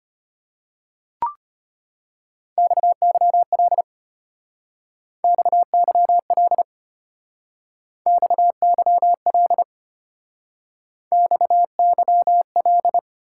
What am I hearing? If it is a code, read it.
Morse code, heard as a keyed tone, about 32 wpm, 700 Hz. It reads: XYL XYL XYL XYL